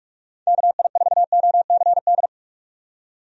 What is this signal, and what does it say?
Morse code, heard as a keyed tone, about 43 wpm, 700 Hz. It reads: KI4OCD
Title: USA Call Signs - 6 Characters - 45wpm